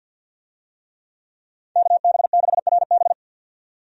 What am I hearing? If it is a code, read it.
Morse code, heard as a keyed tone, about 50 wpm, 700 Hz. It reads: KB6RB
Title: USA Call Signs - 5 Characters - 50wpm